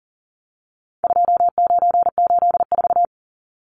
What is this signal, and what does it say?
Morse code, heard as a keyed tone, about 38 wpm, 700 Hz. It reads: 2984
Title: Numbers - 4-Digit 40wpm